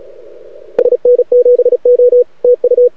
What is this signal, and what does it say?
Morse code, heard as a keyed tone, about 36 wpm, 490 Hz, with radio noise. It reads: SN7O TU
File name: rec contest / arrl cw 2018 / test 32